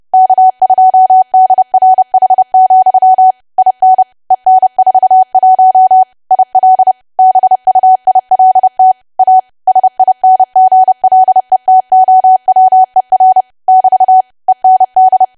Morse code, heard as a keyed tone, about 30 wpm, 750 Hz. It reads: K2DRH, IN EN41 IL BUILT A SINGLE TOWER <BT> END
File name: cwa / W1AW-Level3 / 30WPM